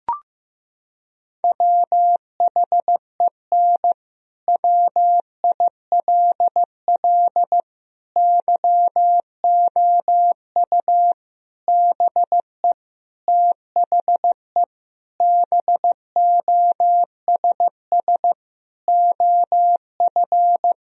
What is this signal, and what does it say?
Morse code, heard as a keyed tone, about 15 wpm, 700 Hz. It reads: WHEN WILL YOU BE THE BOSS OF
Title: Sentences from Top 1000 Words - Review Part 4 - 15wpm